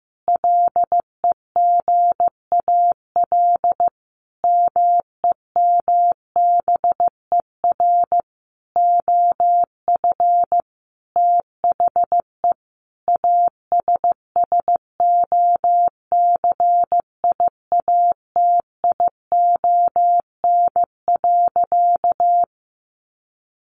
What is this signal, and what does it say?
Morse code, heard as a keyed tone, about 15 wpm, 700 Hz. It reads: LEGAL MEMBER OF THE ASSOCIATION.
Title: Sentences from Top 1200 Words 15wpm